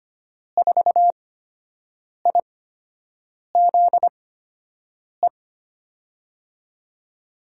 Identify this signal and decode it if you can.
Morse code, heard as a keyed tone, about 25 wpm, spaced as for 7 wpm, 700 Hz. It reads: 4IZE
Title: USA Call Signs - 5 Characters - 8x Character Spacing - 25wpm